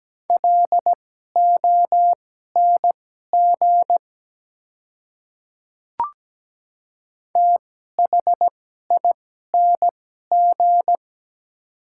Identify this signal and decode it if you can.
Morse code, heard as a keyed tone, about 17 wpm, spaced as for 12 wpm, 700 Hz. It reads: LONG THING
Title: 2x Character Spacing - Top 100 Words - 17wpm